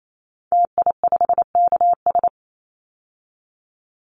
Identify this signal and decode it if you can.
Morse code, heard as a keyed tone, about 28 wpm, 700 Hz. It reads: TI5KS
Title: Call Signs Repeated 3x - 28wpm